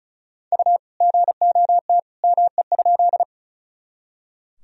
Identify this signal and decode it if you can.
Morse code, heard as a keyed tone, about 34 wpm, 700 Hz. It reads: U GOT ME?
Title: Sentences from Top 300 Words and Abbreviations 35wpm